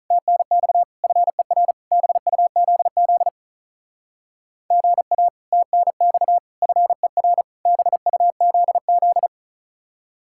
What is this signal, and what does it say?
Morse code, heard as a keyed tone, about 43 wpm, 700 Hz. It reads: TNX FER BUZZ GA TNX FER BUZZ
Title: Speed-Racing Common QSO Phrases - 35wpm